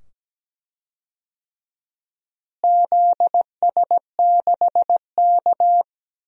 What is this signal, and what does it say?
Morse code, heard as a keyed tone, about 17 wpm, 700 Hz. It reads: ZS6K
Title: DXCC-100 Call Signs - 4 Characters - Repeated 2x - 17wpm